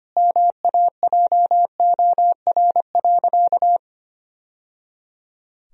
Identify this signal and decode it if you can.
Morse code, heard as a keyed tone, about 25 wpm, 700 Hz. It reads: MAJOR.
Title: Sentences from Top 1000 Words - Review Part 4 - 25wpm